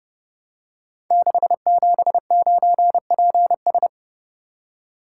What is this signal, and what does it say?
Morse code, heard as a keyed tone, about 30 wpm, 700 Hz. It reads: 679PS
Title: Code Groups - 5 Letter:Numbers - 30wpm